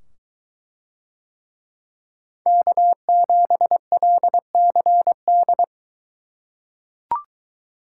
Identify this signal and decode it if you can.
Morse code, heard as a keyed tone, about 22 wpm, 700 Hz. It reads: K7LCD